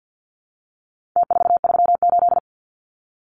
Morse code, heard as a keyed tone, about 50 wpm, 700 Hz. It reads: T438
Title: Numbers - Contest Style - 4-Digit 50wpm